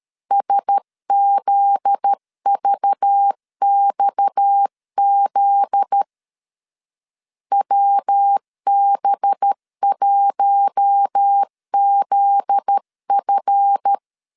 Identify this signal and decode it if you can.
Morse code, heard as a keyed tone, about 13 wpm, 800 Hz. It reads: SZVXZ WB1ZF
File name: LEZIONI / 7 GF9Z / 7 GF9Z 12wpm 2147